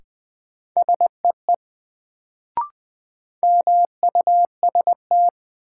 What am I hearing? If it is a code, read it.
Morse code, heard as a keyed tone, about 20 wpm, 700 Hz. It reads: SEE MUST
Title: Sets of 2 Words - New Word Drill - 20wpm